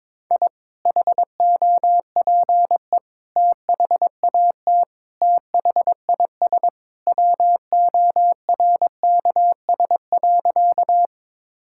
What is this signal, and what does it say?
Morse code, heard as a keyed tone, about 22 wpm, 700 Hz. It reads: I HOPE THAT THIS WORKS.
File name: practice / 22wpm / Sentences with 2-Word Phrases - 22wpm